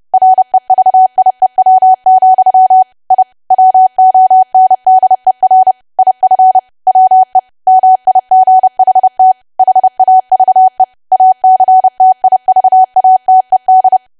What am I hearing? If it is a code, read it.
Morse code, heard as a keyed tone, about 30 wpm, 750 Hz. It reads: REVIEW, I WONDER IF WE MIGHT HAVE ACTIVATED
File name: cwa / W1AW-Level3 / 30WPM